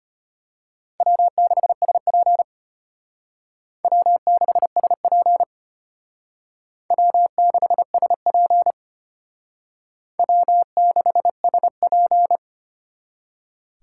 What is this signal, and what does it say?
Morse code, heard as a keyed tone, about 32 wpm, 700 Hz. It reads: W6SP W6SP W6SP W6SP